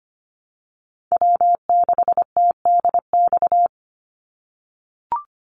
Morse code, heard as a keyed tone, about 25 wpm, 700 Hz. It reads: W6TDX
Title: NA SOTA Chaser Callsigns 25wpm